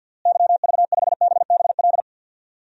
Morse code, heard as a keyed tone, about 52 wpm, 700 Hz. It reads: YV5BBL
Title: DXCC-100 Call Signs - 6 Characters - Repeated 3x - 50wpm